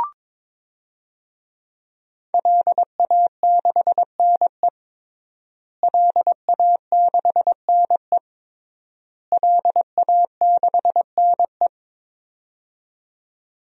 Morse code, heard as a keyed tone, about 22 wpm, 700 Hz. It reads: LA6NE LA6NE LA6NE
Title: DXCC-100 Call Signs - 5 Characters - Repeated 3x - 22wpm